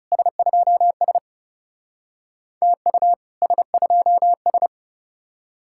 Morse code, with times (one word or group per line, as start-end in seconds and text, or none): 0.1-1.2 S2S
2.6-3.1 TU
3.4-4.7 S2S